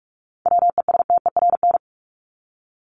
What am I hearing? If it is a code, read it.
Morse code, heard as a keyed tone, about 44 wpm, 700 Hz. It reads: WESTERN
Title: Top 751-1000 Words 45wpm